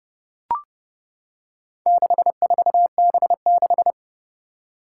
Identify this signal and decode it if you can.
Morse code, heard as a keyed tone, about 30 wpm, 700 Hz. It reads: 64B6
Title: Four Challenging Letter-Number Combinations 30wpm